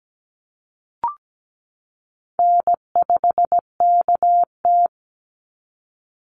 Morse code, heard as a keyed tone, about 17 wpm, 700 Hz. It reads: N5KT
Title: Easy Call Signs Repeated 2x - 17wpm